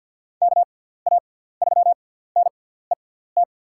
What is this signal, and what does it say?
Morse code, heard as a keyed tone, about 51 wpm, spaced as for 17 wpm, 700 Hz. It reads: KA3NET
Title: USA Call Signs - 6 Characters - 6x Character Spacing - 50wpm